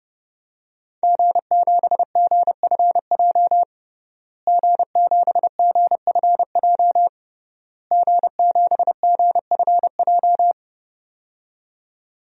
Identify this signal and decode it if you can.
Morse code, heard as a keyed tone, about 31 wpm, 700 Hz. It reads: G7GFJ G7GFJ G7GFJ
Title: DXCC-100 Call Signs - 5 Characters - Repeated 3x - 30wpm